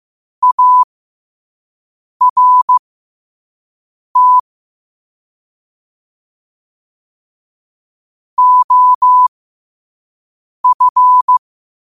Morse code, heard as a keyed tone, about 15 wpm, spaced as for 5 wpm, 1000 Hz. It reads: ART OF
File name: 5wpm